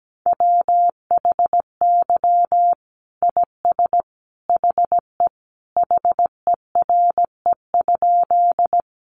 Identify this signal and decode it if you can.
Morse code, heard as a keyed tone, about 17 wpm, 700 Hz. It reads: WHY IS HE HERE?
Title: Sentences from Top 200 Words 17wpm